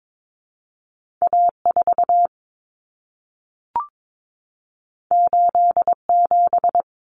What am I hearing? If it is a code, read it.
Morse code, heard as a keyed tone, about 22 wpm, 700 Hz. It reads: A4 87
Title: Common Cut Numbers - 2-Digit 22wpm